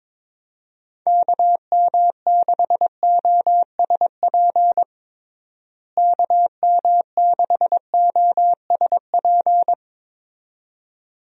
Morse code, heard as a keyed tone, about 22 wpm, 700 Hz. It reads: KM6OSP KM6OSP
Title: USA Call Signs - 6 Characters - Repeated 2x - 22wpm